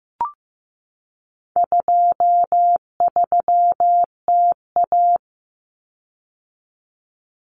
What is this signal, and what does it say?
Morse code, heard as a keyed tone, about 15 wpm, 700 Hz. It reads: 23TA
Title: Common Cut Numbers - 4-Digit 15wpm